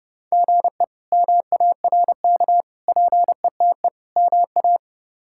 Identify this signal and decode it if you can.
Morse code, heard as a keyed tone, about 30 wpm, 700 Hz. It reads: GE MARK PETE MA
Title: K1USN SST QSO Practice - 30wpm